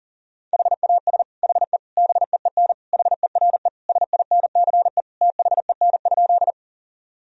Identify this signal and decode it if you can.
Morse code, heard as a keyed tone, about 40 wpm, 700 Hz. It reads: HAS HE BEEN HERE SINCE THEN?
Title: Sentences from Top 1000 Words - Review Part 1 - 40wpm